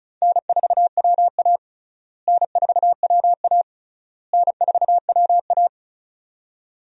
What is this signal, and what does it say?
Morse code, heard as a keyed tone, about 36 wpm, 700 Hz. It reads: N4WA N4WA N4WA
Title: USA Call Signs - 4 Characters - Repeated 3x - 35wpm